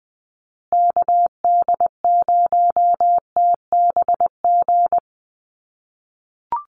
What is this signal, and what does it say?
Morse code, heard as a keyed tone, about 20 wpm, 700 Hz. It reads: KD0TBG
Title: USA Call Signs Repeated 3x 20wpm